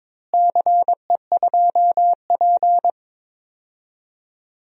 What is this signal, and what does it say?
Morse code, heard as a keyed tone, about 22 wpm, 700 Hz. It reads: CE2P
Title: DXCC-100 Call Signs - 4 Characters - Repeated 3x - 22wpm